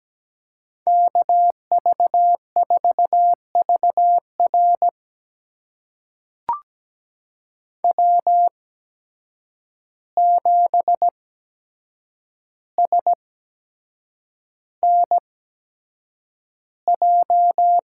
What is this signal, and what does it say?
Morse code, heard as a keyed tone, about 17 wpm, spaced as for 7 wpm, 700 Hz. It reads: KV4VR W7SNJ
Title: USA Call Signs - 5 Characters - 8x Character Spacing - 17wpm